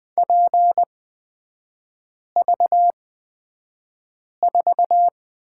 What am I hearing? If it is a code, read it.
Morse code, heard as a keyed tone, about 20 wpm, spaced as for 5 wpm, 700 Hz. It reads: PV4
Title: 255 Lesson 41e - Period - Three Characters 20wpm